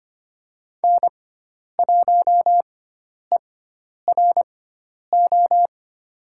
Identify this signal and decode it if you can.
Morse code, heard as a keyed tone, about 25 wpm, spaced as for 10 wpm, 700 Hz. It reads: N1ERO
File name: Lesson 16g - Letter c - Call Signs 25wpm